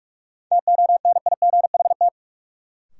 Morse code, heard as a keyed tone, about 46 wpm, 700 Hz. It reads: TONIGHT
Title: Sentences from Top 600 Words - New Word Drill - 45wpm